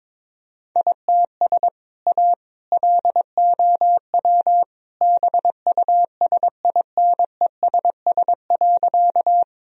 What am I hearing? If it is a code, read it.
Morse code, heard as a keyed tone, about 22 wpm, 700 Hz. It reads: ITS A LOW BUSINESS.
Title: Sentences from Top 1000 Words - Review Part 1 - 22wpm